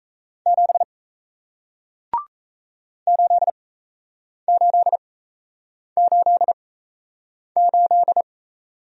Morse code, heard as a keyed tone, about 37 wpm, spaced as for 8 wpm, 700 Hz. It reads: 7 8888